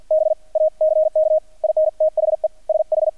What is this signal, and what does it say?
Morse code, heard as a keyed tone, about 48 wpm, spaced as for 41 wpm, 630 Hz. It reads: BACK IN THE SH
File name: archived / CRSnet